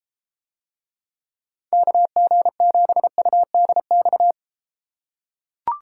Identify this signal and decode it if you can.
Morse code, heard as a keyed tone, about 32 wpm, 700 Hz. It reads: KG7UDX